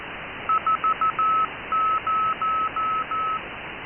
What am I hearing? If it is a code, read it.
Morse code, heard as a keyed tone, about 14 wpm, 1280 Hz. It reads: 40